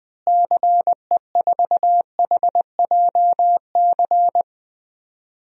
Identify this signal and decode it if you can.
Morse code, heard as a keyed tone, about 20 wpm, 700 Hz. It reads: CE4HJC
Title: DXCC-100 Call Signs - 6 Characters - Repeated 2x - 20wpm